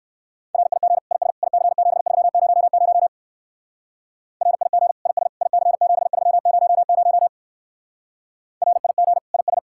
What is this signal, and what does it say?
Morse code, heard as a keyed tone, about 56 wpm, 700 Hz. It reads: RIG IS IC7300 RIG IS IC7300 RIG IS
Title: Speed-Racing Common QSO Phrases - 45wpm